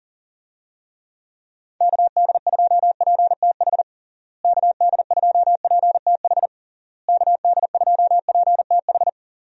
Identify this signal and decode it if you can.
Morse code, heard as a keyed tone, about 40 wpm, 700 Hz. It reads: KD2PTH KD2PTH KD2PTH
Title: USA Call Signs - 6 Characters - Repeated 3x - 40wpm